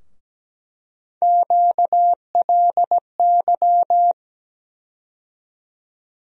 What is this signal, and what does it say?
Morse code, heard as a keyed tone, about 17 wpm, 700 Hz. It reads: QLY
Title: USA Call Sign Postfixes - 3 Letters 17wpm